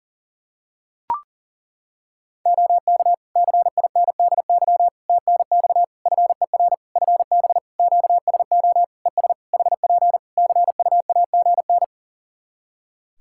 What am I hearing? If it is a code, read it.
Morse code, heard as a keyed tone, about 40 wpm, 700 Hz. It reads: OK CINDY TNX FER FB QSO ES HP CUAGN
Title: K1USN SST QSO Practice 40wpm